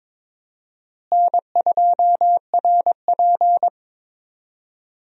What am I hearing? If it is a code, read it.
Morse code, heard as a keyed tone, about 22 wpm, 700 Hz. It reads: N2RP
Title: USA Call Signs - 4 Characters - Repeated 3x - 22wpm